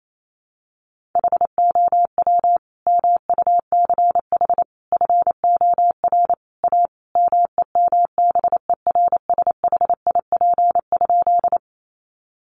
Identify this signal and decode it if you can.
Morse code, heard as a keyed tone, about 28 wpm, 700 Hz. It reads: HOW MUCH FOR A MEMBERSHIP?
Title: Sentences from Top 1400 Words 28wpm